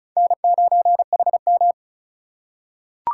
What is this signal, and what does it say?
Morse code, heard as a keyed tone, about 34 wpm, 700 Hz. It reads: N9HM
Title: USA Call Signs - 4 Characters - 3x Character Spacing - 35wpm